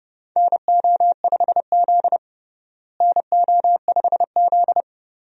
Morse code, heard as a keyed tone, about 31 wpm, 700 Hz. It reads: NO5Z NO5Z
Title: USA Call Signs - 4 Characters - Repeated 3x - 30wpm